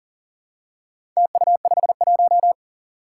Morse code, heard as a keyed tone, about 37 wpm, 700 Hz. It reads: TU51
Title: Four Challenging Letter-Number Combinations 40wpm